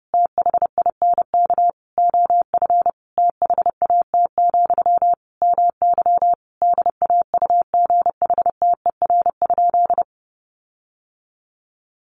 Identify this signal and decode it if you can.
Morse code, heard as a keyed tone, about 30 wpm, 700 Hz. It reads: THINK OF THAT, MY DAUGHTER?
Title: Sentences from Top 1500 Words 30wpm